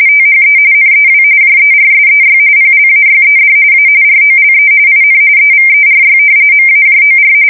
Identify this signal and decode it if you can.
RTTY, about 45.45 baud. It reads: THE NEW JERSEY PLAYED A ROLE IN EVERY MAJOR
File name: fsk170 / battleshipnj